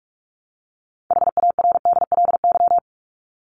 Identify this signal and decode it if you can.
Morse code, heard as a keyed tone, about 44 wpm, 700 Hz. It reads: HARDLY
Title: Top 1751-2000 Words 45wpm